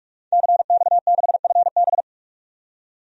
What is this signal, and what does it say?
Morse code, heard as a keyed tone, about 46 wpm, 700 Hz. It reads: CX6FB
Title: DXCC-100 Call Signs - 5 Characters - 45wpm